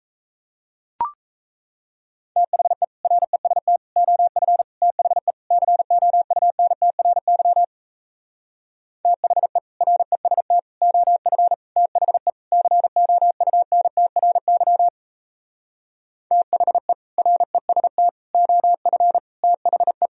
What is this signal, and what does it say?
Morse code, heard as a keyed tone, about 37 wpm, 700 Hz. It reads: THE REST OF THE COUNTRY THE REST OF THE COUNTRY THE REST OF THE